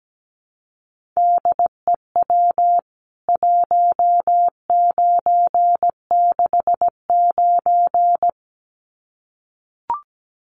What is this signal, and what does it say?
Morse code, heard as a keyed tone, about 17 wpm, 700 Hz. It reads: DEW 1969
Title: CWT Exchange 17wpm